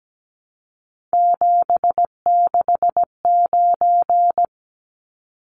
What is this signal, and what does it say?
Morse code, heard as a keyed tone, about 17 wpm, 700 Hz. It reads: 769
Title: Numbers - 3-Digit 17wpm